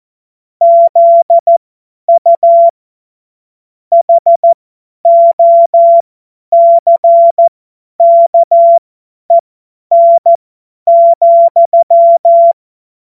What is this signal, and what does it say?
Morse code, heard as a keyed tone, about 14 wpm, spaced as for 10 wpm, 680 Hz. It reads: ZU HOCKEN,
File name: Morsekurs / Witze / Witz 27. September 2022 50 F70